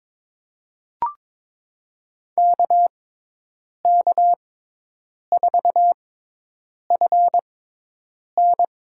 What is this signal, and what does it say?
Morse code, heard as a keyed tone, about 22 wpm, spaced as for 8 wpm, 700 Hz. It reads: KK4FN
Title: USA Call Signs - 6x Character Spacing - 22wpm